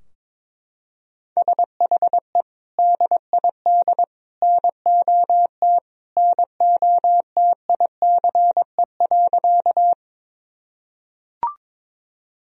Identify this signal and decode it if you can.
Morse code, heard as a keyed tone, about 22 wpm, 700 Hz. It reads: SHE DID NOT NOTICE.
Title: Sentences from Top 1000 Words - Review Part 2 - 22wpm